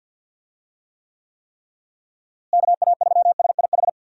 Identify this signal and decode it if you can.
Morse code, heard as a keyed tone, about 52 wpm, 700 Hz. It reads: KA3SIH
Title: USA Call Signs - 6 Characters - Repeated 3x - 50wpm